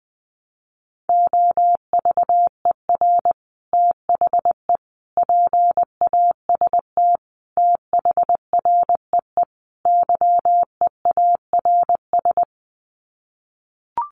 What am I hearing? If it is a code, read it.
Morse code, heard as a keyed tone, about 20 wpm, 700 Hz. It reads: OVER THE PAST THREE YEARS
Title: Sets of 5 Words Encore2 20wpm